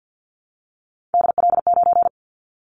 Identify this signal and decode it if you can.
Morse code, heard as a keyed tone, about 50 wpm, 700 Hz. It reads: DL9
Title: Callsign Prefixes for Top 50 DXCC - 50wpm